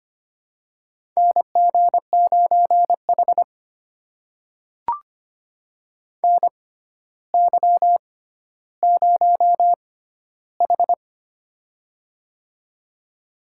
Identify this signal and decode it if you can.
Morse code, heard as a keyed tone, about 25 wpm, spaced as for 13 wpm, 700 Hz. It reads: NG9H NY0H
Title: USA Call Signs - 4 Characters - 6x Character Spacing - 25wpm